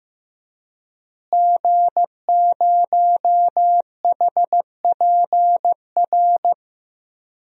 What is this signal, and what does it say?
Morse code, heard as a keyed tone, about 15 wpm, 700 Hz. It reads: G0HPR
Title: DXCC-100 Call Signs - 5 Characters - Repeated 3x - 15wpm